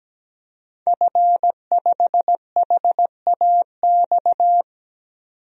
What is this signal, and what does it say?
Morse code, heard as a keyed tone, about 17 wpm, 700 Hz. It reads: F5HAX
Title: DXCC-100 Call Signs - 5 Characters - 17wpm